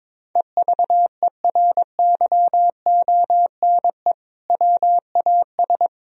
Morse code, heard as a keyed tone, about 22 wpm, 700 Hz. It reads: EVERYONE WAS
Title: Sentences from Top 1000 Words - Review Part 3 - 22wpm